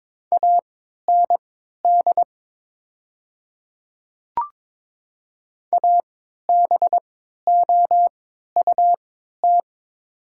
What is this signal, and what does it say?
Morse code, heard as a keyed tone, about 22 wpm, spaced as for 13 wpm, 700 Hz. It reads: AND ABOUT
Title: 3x Character Spacing - Top 500 Words - 22wpm